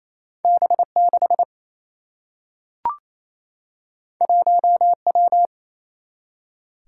Morse code, heard as a keyed tone, about 28 wpm, 700 Hz. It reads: B 6 1 W
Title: Single-Letter Single-Number 28wpm